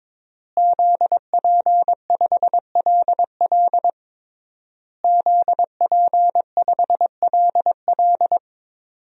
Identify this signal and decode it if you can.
Morse code, heard as a keyed tone, about 22 wpm, 700 Hz. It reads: ZP5LL ZP5LL
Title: DXCC-100 Call Signs - 5 Characters - Repeated 3x - 22wpm